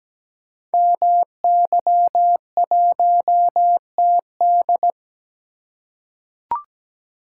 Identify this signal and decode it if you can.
Morse code, heard as a keyed tone, about 17 wpm, 700 Hz. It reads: MY1TD
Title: Code Groups - 5 Letter:Numbers - 17wpm